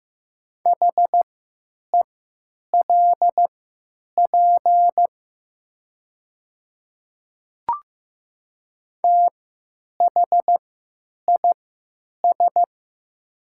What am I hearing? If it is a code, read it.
Morse code, heard as a keyed tone, about 15 wpm, spaced as for 8 wpm, 700 Hz. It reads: HELP THIS
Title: 3x Character Spacing - Top 300 Words - 15wpm